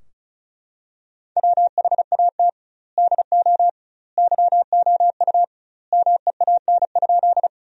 Morse code, heard as a keyed tone, about 34 wpm, 700 Hz. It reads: WHAT DO YOU MEAN?
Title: 2x Word-Spacing - Sentences from Top 200 Words - 35wpm